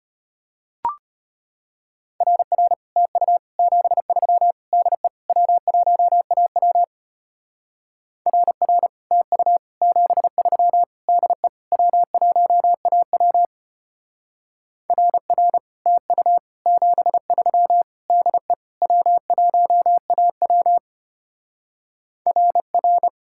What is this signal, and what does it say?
Morse code, heard as a keyed tone, about 32 wpm, 700 Hz. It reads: RR TU 73 DE W1AW RR TU 73 DE W1AW RR TU 73 DE W1AW RR